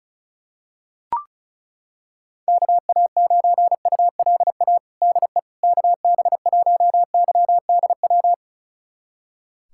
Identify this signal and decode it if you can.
Morse code, heard as a keyed tone, about 34 wpm, 700 Hz. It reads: KA9ULA DE KB1YDW
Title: USA Callsign DE Callsign - 6 Characters - 35wpm